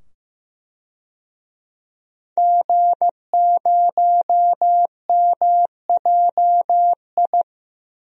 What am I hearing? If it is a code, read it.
Morse code, heard as a keyed tone, about 15 wpm, 700 Hz. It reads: G0MJI
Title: DXCC-100 Call Signs - 5 Characters - 15wpm